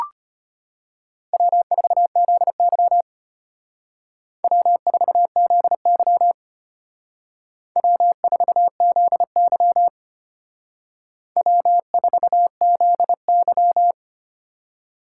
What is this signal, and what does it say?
Morse code, heard as a keyed tone, about 32 wpm, 700 Hz. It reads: W4ZY W4ZY W4ZY W4ZY